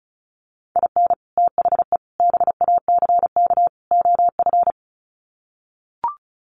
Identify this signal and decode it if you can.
Morse code, heard as a keyed tone, about 34 wpm, 700 Hz. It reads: IN THE BACK OF
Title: Sets of 4 Words 35wpm